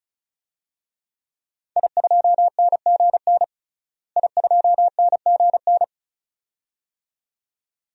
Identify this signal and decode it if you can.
Morse code, heard as a keyed tone, about 36 wpm, 700 Hz. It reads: I2NGN I2NGN
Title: DXCC-100 Call Signs - 5 Characters - Repeated 2x - 35wpm